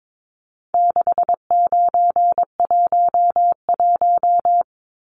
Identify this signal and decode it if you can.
Morse code, heard as a keyed tone, about 22 wpm, 700 Hz. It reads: 6911
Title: Numbers - 4-Digit 22wpm